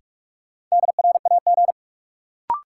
Morse code, heard as a keyed tone, about 46 wpm, 700 Hz. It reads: DRAG
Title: Sentences from Top 1500 Words - New Word Drill Alt - 45wpm